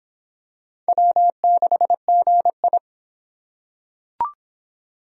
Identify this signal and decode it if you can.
Morse code, heard as a keyed tone, about 26 wpm, 700 Hz. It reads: W6GI